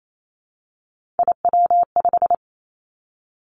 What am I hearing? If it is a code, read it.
Morse code, heard as a keyed tone, about 28 wpm, 690 Hz. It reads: IW5
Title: Three Challenging Letter-Number Combinations 28wpm